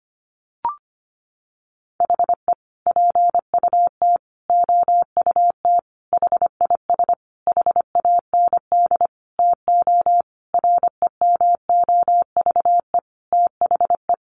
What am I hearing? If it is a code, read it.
Morse code, heard as a keyed tone, about 25 wpm, 700 Hz. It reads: HE PUT OUT HIS HAND TO REMOVE THE
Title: Sentences from Top 800 Words 25wpm